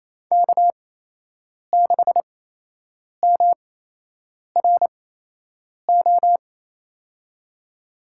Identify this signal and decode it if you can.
Morse code, heard as a keyed tone, about 27 wpm, spaced as for 8 wpm, 700 Hz. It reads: K6MRO
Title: USA Call Signs - 8x Character Spacing - 28wpm